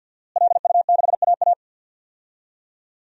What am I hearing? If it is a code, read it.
Morse code, heard as a keyed tone, about 48 wpm, 700 Hz. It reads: LU6AA
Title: DXCC-100 Call Signs - 5 Characters - Repeated 3x - 50wpm